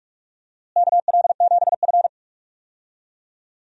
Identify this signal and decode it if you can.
Morse code, heard as a keyed tone, about 44 wpm, 700 Hz. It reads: KL7F
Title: DXCC-100 Call Signs - 4 Characters - Repeated 3x - 45wpm